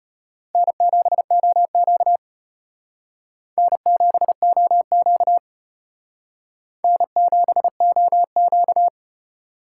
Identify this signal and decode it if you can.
Morse code, heard as a keyed tone, about 32 wpm, 700 Hz. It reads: N7OQ N7OQ N7OQ